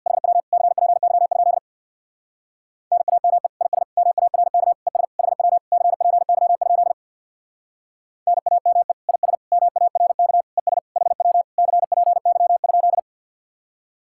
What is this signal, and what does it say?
Morse code, heard as a keyed tone, about 64 wpm, 700 Hz. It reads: HW CPY? NAME IS MARK ES HW CPY? NAME IS MARK ES HW CPY?